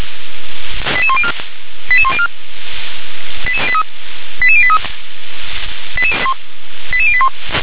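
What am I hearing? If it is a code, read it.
Five-tone selective call (ZVEI), heard as quick runs of five tones, tones of about 70 ms. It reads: 88104 88104 88083 88081
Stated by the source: ZVEI five-tone selective call received by SDR.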